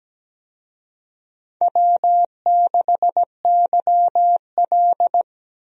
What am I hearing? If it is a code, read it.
Morse code, heard as a keyed tone, about 17 wpm, 700 Hz. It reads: W6YL
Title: USA Call Signs - 4 Characters - 17wpm